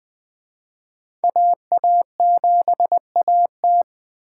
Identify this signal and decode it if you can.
Morse code, heard as a keyed tone, about 24 wpm, 700 Hz. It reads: AA7AT
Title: Speed Racing - USA 2x2 Call Signs 20wpm